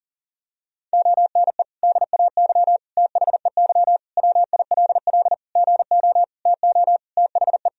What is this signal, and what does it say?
Morse code, heard as a keyed tone, about 40 wpm, 700 Hz. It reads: ONE DAY THEY WILL GO TO THE
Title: Sentences from Top 1000 Words - Review Part 1 - 40wpm